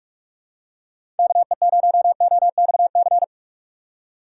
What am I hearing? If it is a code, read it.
Morse code, heard as a keyed tone, about 46 wpm, 700 Hz. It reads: KE0OXC
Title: USA Call Signs - 6 Characters - 3x Character Spacing - 45wpm